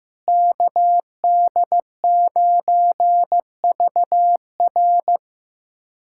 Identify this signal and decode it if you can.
Morse code, heard as a keyed tone, about 15 wpm, 700 Hz. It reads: KD9VR
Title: USA Callsign DE Callsign - 5 Characters - 15wpm